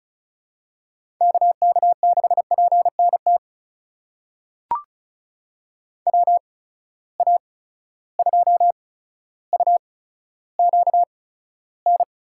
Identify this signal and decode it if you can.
Morse code, heard as a keyed tone, about 34 wpm, spaced as for 15 wpm, 700 Hz. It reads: KK6PNT WA2UQN
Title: USA Call Signs - 8x Character Spacing - 35wpm